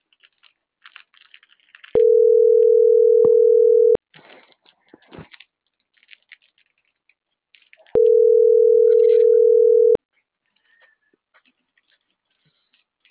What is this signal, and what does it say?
Ringback tone (North America), 440 + 480 Hz.
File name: CENTRAL MAERSK / out luis.herrera - .30547